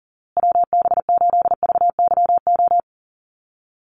Morse code, heard as a keyed tone, about 40 wpm, 700 Hz. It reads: WB8VYO
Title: USA Call Signs Repeated 3x 40wpm